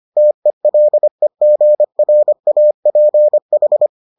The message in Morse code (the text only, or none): TELEGRAPH